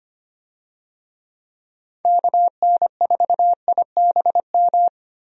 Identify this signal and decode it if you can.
Morse code, heard as a keyed tone, about 25 wpm, 700 Hz. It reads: KN4IBM
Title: USA Call Signs - 6 Characters - Repeated 2x - 25wpm